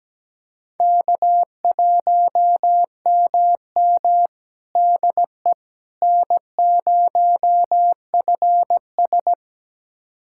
Morse code, heard as a keyed tone, about 17 wpm, 700 Hz. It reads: K1MM DE N0FS
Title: USA Callsign DE Callsign - 4 Characters - 17wpm